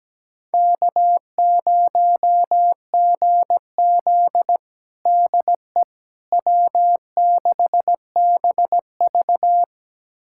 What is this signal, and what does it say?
Morse code, heard as a keyed tone, about 17 wpm, 700 Hz. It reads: K0GZ DE W6BV
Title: USA Callsign DE Callsign - 4 Characters - 17wpm